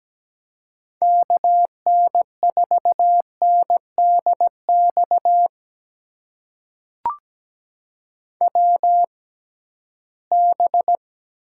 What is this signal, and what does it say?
Morse code, heard as a keyed tone, about 17 wpm, spaced as for 13 wpm, 700 Hz. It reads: KN4NDX WB
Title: USA Call Signs - 6 Characters - 6x Character Spacing - 17wpm